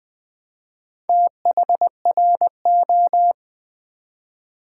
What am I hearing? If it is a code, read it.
Morse code, heard as a keyed tone, about 20 wpm, 700 Hz. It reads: THRO
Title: Build A Word - 1 Letter At A Time - 5-Letter Words - 20wpm